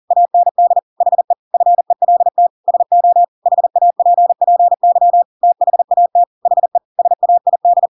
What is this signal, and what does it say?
Morse code, heard as a keyed tone, about 40 wpm, 700 Hz. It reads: AND HE FELT SO HAPPY THAT HE SAID
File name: Pinocchio / Pinocchio ch8 40 40 ik1hge